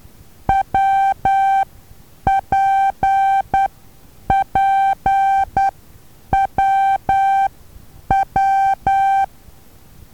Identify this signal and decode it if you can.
Morse code, heard as a keyed tone, about 10 wpm, spaced as for 8 wpm, 780 Hz. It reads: WPPWW
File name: Taviro HA6VA B